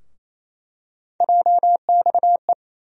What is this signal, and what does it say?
Morse code, heard as a keyed tone, about 28 wpm, 700 Hz. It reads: JXE
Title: USA Call Sign Postfixes - 3 Letters 28wpm